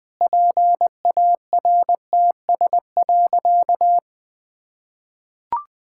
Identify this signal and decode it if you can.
Morse code, heard as a keyed tone, about 20 wpm, 700 Hz. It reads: PARTS.
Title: Sentences with 7-Word Phrases - Spoken Beginning - 20wpm